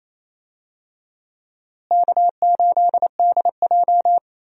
Morse code, heard as a keyed tone, about 27 wpm, 700 Hz. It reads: K8DJ
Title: USA Call Signs - 4 Characters - Repeated 2x - 28wpm